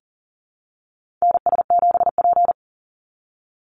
Morse code, heard as a keyed tone, about 40 wpm, 700 Hz. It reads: NS7P
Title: NA SOTA Activator Callsigns 40wpm